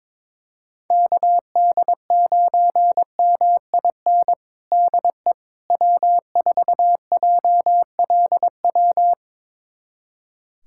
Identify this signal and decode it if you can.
Morse code, heard as a keyed tone, about 21 wpm, 700 Hz. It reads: KD9MIN DE W4JLW
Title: USA Callsign DE Callsign 22wpm